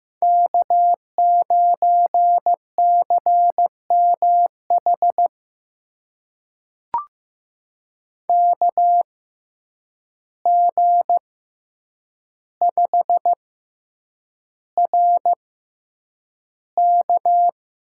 Morse code, heard as a keyed tone, about 15 wpm, spaced as for 8 wpm, 700 Hz. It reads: K9CMH KG5RK
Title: USA Call Signs - 5 Characters - 6x Character Spacing - 15wpm